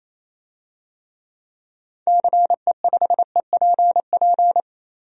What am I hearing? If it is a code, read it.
Morse code, heard as a keyed tone, about 28 wpm, 700 Hz. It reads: CE5EPP
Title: DXCC-100 Call Signs - 6 Characters - Repeated 3x - 28wpm